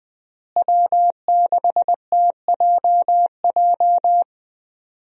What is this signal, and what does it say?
Morse code, heard as a keyed tone, about 20 wpm, 700 Hz. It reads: W6TJJ
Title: USA Call Signs - 5 Characters - Repeated 3x - 20wpm